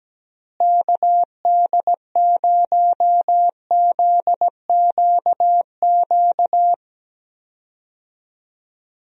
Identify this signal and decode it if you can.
Morse code, heard as a keyed tone, about 17 wpm, 700 Hz. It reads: KD0ZQQ
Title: USA Call Signs - 6 Characters - Repeated 2x - 17wpm